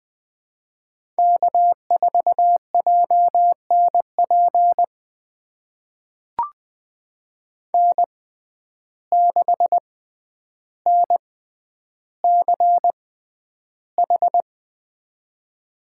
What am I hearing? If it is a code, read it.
Morse code, heard as a keyed tone, about 20 wpm, spaced as for 10 wpm, 700 Hz. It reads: K4JNP N6NCH
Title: USA Call Signs - 6x Character Spacing - 20wpm